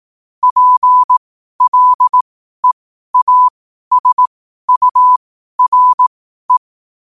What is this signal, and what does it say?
Morse code, heard as a keyed tone, about 18 wpm, spaced as for 12 wpm, 1000 Hz. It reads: PLEASURE
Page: New quotes every day in morse code at 12 Words per minute.